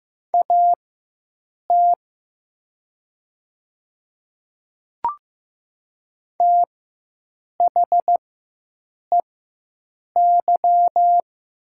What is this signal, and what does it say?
Morse code, heard as a keyed tone, about 15 wpm, spaced as for 7 wpm, 700 Hz. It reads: AT THEY
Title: 4x Character Spacing - Top 100 Words - 15wpm